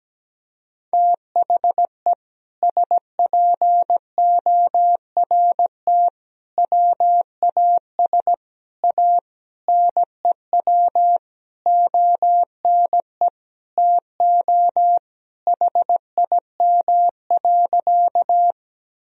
Morse code, heard as a keyed tone, about 17 wpm, 700 Hz. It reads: THE SPORT WAS A NEW ONE TO HIM.
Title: Sentences from Top 1000 Words - Review Part 4 - 17wpm